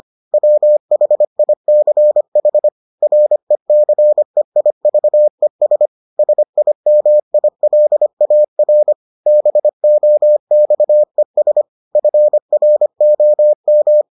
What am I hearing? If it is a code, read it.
Morse code, heard as a keyed tone, about 25 wpm, 600 Hz. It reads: WHICH RECEIVES SIMILAR BOXES FROM